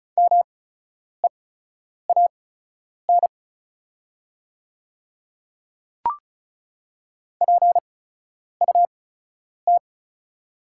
Morse code, heard as a keyed tone, about 36 wpm, spaced as for 10 wpm, 700 Hz. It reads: MEAN PUT
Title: 8x Character Spacing - Top 200 Words - 35wpm